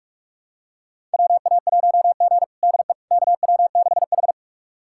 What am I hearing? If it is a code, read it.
Morse code, heard as a keyed tone, about 42 wpm, 700 Hz. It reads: WA1G DE KW6H
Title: USA Callsign DE Callsign - 4 Characters - 45wpm